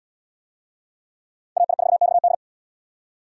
Morse code, heard as a keyed tone, about 68 wpm, 700 Hz. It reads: AI4XD